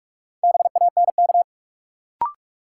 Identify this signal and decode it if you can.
Morse code, heard as a keyed tone, about 47 wpm, 700 Hz. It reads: BANK
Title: Sets of 5 Words - New Word Drill - 45wpm